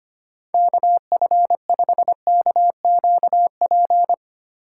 Morse code, heard as a keyed tone, about 25 wpm, 700 Hz. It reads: KF5KQP
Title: USA Call Signs - 6 Characters - 3x Character Spacing - 25wpm